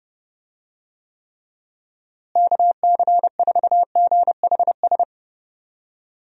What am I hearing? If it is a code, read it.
Morse code, heard as a keyed tone, about 30 wpm, 700 Hz. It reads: KC4GHS
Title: USA Call Signs - 6 Characters - Repeated 2x - 30wpm